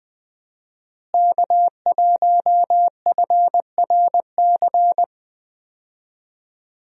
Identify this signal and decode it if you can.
Morse code, heard as a keyed tone, about 20 wpm, 700 Hz. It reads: K1FRC
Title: USA Call Signs - 5 Characters - Repeated 2x - 20wpm